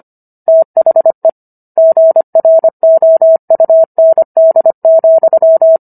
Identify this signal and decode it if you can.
Morse code, heard as a keyed tone, about 25 wpm, 650 Hz. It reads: THE GROUND,